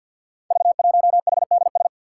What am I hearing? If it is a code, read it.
Morse code, heard as a keyed tone, about 51 wpm, 700 Hz. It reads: V1HDS
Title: Five Challenging Letter-Number Combinations 50wpm